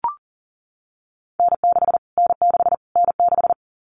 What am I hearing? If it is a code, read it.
Morse code, heard as a keyed tone, about 40 wpm, 700 Hz. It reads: N6 N6 N6
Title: Callsign Prefixes for Top 25 DXCC - Repeated 3x - 40wpm